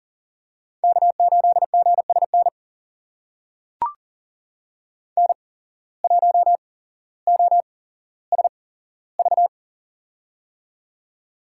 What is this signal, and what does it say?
Morse code, heard as a keyed tone, about 40 wpm, spaced as for 17 wpm, 700 Hz. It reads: K8GSN N1OSV
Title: USA Call Signs - 5 Characters - 8x Character Spacing - 40wpm